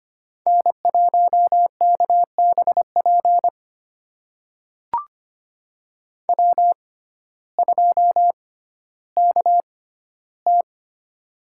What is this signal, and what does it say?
Morse code, heard as a keyed tone, about 25 wpm, spaced as for 14 wpm, 700 Hz. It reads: N1KBP W2KT
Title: USA Call Signs - 6x Character Spacing - 25wpm